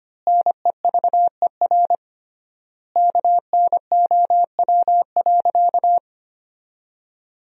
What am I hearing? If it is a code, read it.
Morse code, heard as a keyed tone, about 25 wpm, 700 Hz. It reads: NEVER KNOW.
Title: 3x Word-Spacing - Sentences from Top 300 Words - 25wpm